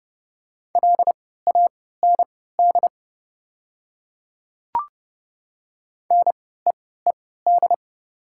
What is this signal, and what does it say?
Morse code, heard as a keyed tone, about 30 wpm, spaced as for 17 wpm, 690 Hz. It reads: LAND NEED
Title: 3x Character Spacing - Top 200 Words - 30wpm